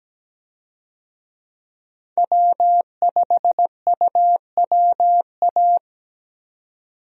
Morse code, heard as a keyed tone, about 17 wpm, 700 Hz. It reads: W5UWA
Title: USA Call Signs - 5 Characters - 17wpm